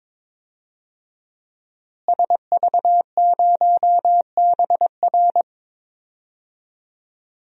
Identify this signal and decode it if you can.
Morse code, heard as a keyed tone, about 23 wpm, 700 Hz. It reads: SV0BR
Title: DXCC-100 Call Signs - 5 Characters - Repeated 3x - 22wpm